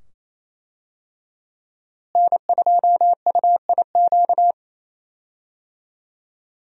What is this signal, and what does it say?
Morse code, heard as a keyed tone, about 28 wpm, 700 Hz. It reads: N2UIQ
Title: USA Call Signs - 5 Characters - Repeated 2x - 28wpm